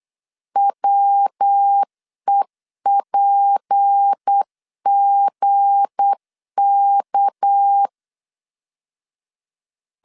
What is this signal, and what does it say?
Morse code, heard as a keyed tone, about 8 wpm, 800 Hz. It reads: WEPGK
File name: LEZIONI / 8 YJ45 / 8 YJ4 8wpm 3246